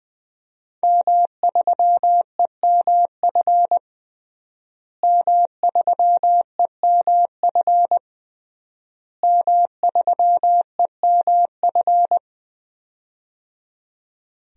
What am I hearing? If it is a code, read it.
Morse code, heard as a keyed tone, about 20 wpm, 700 Hz. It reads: M3EMF M3EMF M3EMF
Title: DXCC-100 Call Signs - 5 Characters - Repeated 3x - 20wpm